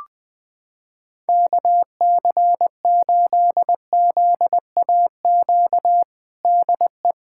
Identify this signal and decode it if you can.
Morse code, heard as a keyed tone, about 20 wpm, 700 Hz. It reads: KC8ZAQ DE
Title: USA Callsign DE Callsign 20wpm